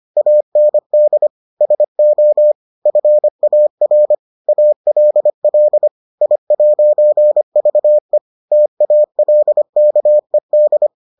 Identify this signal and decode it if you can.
Morse code, heard as a keyed tone, about 25 wpm, 600 Hz. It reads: AND SO FAR ALL I'VE TALKED